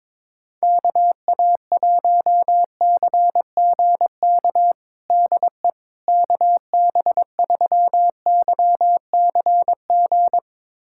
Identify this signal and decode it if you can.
Morse code, heard as a keyed tone, about 21 wpm, 700 Hz. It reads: KA1CGK DE KB3YCG
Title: USA Callsign DE Callsign - 6 Characters - 22wpm